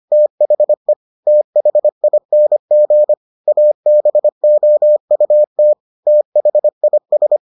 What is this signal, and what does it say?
Morse code, heard as a keyed tone, about 25 wpm, 600 Hz. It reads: THE THING ABOUT THIS